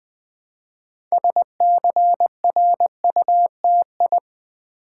Morse code, heard as a keyed tone, about 20 wpm, 700 Hz. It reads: SCRUTI
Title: Build A Word - 3 Letters At A Time - 8-Letter Words - 20wpm